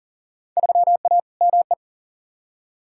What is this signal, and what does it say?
Morse code, heard as a keyed tone, about 40 wpm, 700 Hz. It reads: 3A ME
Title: Field Day Exchanges - 40wpm